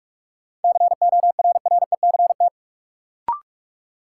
Morse code, heard as a keyed tone, about 43 wpm, 700 Hz. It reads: CORRECT
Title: Sentences from Top 700 Words - New Word Drill - 45wpm